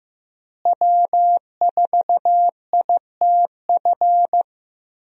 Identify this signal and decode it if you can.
Morse code, heard as a keyed tone, about 14 wpm, 700 Hz. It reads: W4ITF
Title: USA Call Signs - 5 Characters - Repeated 3x - 15wpm